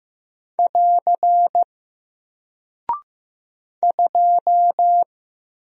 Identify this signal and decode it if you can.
Morse code, heard as a keyed tone, about 15 wpm, 700 Hz. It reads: <AR> 2
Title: Single Character (Letter:Number:Punctuation:Prosign) Alt 15wpm